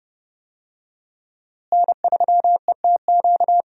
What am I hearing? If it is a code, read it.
Morse code, heard as a keyed tone, about 30 wpm, 700 Hz. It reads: N3ETQ
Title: USA Call Signs - 5 Characters - Repeated 2x - 30wpm